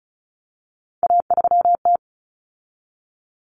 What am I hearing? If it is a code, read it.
Morse code, heard as a keyed tone, about 34 wpm, 700 Hz. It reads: A3T
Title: Common Cut Numbers - 3-Digit 35wpm